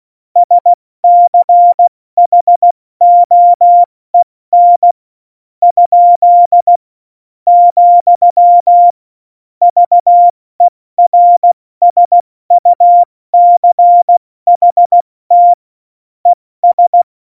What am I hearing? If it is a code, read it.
Morse code, heard as a keyed tone, about 16 wpm, 700 Hz. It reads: SCHOEN ? , VERSUCHT ES
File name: Morsekurs / Witze / Witz 14. August 2018, 70 F80